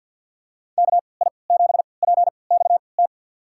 Morse code, heard as a keyed tone, about 52 wpm, spaced as for 29 wpm, 700 Hz. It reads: KI7PXT
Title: USA Call Signs - 6 Characters - 3x Character Spacing - 50wpm